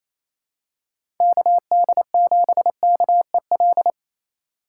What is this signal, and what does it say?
Morse code, heard as a keyed tone, about 28 wpm, 700 Hz. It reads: KD7KEL
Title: USA Call Signs - 3x Character Spacing - 28wpm